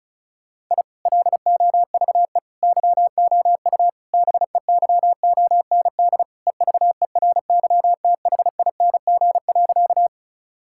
Morse code, heard as a keyed tone, about 36 wpm, 700 Hz. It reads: I LOVE YOU BEYOND EVERYTHING.
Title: Sentences from Top 1000 Words - Review Part 6 - 35wpm